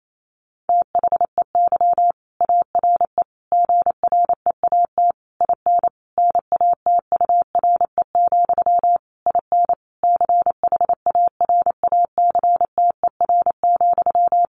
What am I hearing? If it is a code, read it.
Morse code, heard as a keyed tone, about 28 wpm, 700 Hz. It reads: THEY ARE GREAT IN NATURE, IN CHARACTER,
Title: Sentences from Top 1400 Words 28wpm